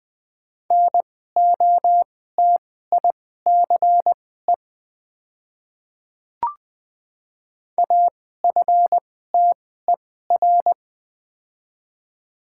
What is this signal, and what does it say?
Morse code, heard as a keyed tone, about 20 wpm, spaced as for 15 wpm, 700 Hz. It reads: NOTICE AFTER
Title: 2x Character Spacing - Top 500 Words - 20wpm